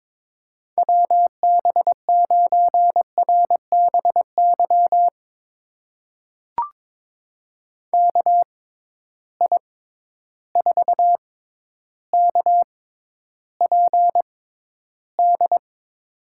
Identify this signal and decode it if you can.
Morse code, heard as a keyed tone, about 22 wpm, spaced as for 11 wpm, 700 Hz. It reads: WB9RBY KI4KPD
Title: USA Call Signs - 6 Characters - 6x Character Spacing - 22wpm